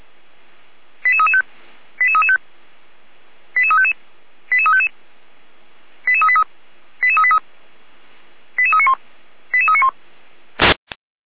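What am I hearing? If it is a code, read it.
Five-tone selective call (ZVEI), heard as quick runs of five tones, tones of about 70 ms. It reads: 88335 88335 88355 88355 88383 88383 88391 88391
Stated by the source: ZVEI five-tone selective call 88367 received on an SDR.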